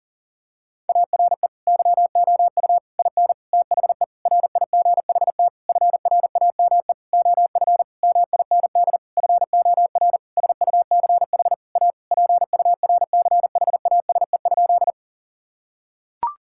Morse code, heard as a keyed tone, about 40 wpm, 700 Hz. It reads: ARE YOU IN THE RIGHT FRAME OF MIND FOR SUCH A PURCHASE?
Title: Sentences from Top 1000 Words - Review Part 6 - 40wpm